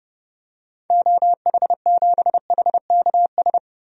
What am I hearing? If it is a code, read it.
Morse code, heard as a keyed tone, about 30 wpm, 700 Hz. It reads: OH7HKS
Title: DXCC-100 Call Signs - 6 Characters - Repeated 3x - 30wpm